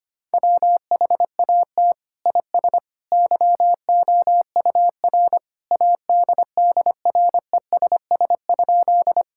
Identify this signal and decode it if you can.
Morse code, heard as a keyed tone, about 25 wpm, 700 Hz. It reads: WHAT IS YOUR ADDRESS?
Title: Sentences from Top 1000 Words - Review Part 6 - 25wpm